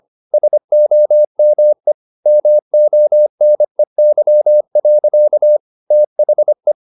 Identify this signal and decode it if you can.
Morse code, heard as a keyed tone, about 25 wpm, 600 Hz. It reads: SOME MONEY. THE
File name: the old man and the sea part 1 25wpm